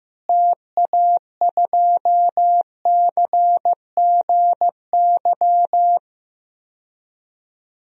Morse code, heard as a keyed tone, about 15 wpm, 700 Hz. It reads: TA2CGY
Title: DXCC-100 Call Signs - 6 Characters - Repeated 2x - 15wpm